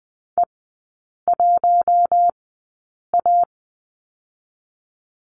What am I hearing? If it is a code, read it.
Morse code, heard as a keyed tone, about 20 wpm, spaced as for 8 wpm, 700 Hz. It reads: E1A
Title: 053 Lesson 10e - Letter r - Three Characters 20wpm